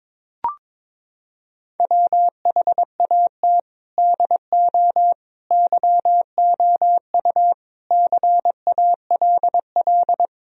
Morse code, heard as a keyed tone, about 22 wpm, 700 Hz. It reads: WHAT DO YOU CALL
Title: Sentences from Top 1000 Words - Review Part 3 - 22wpm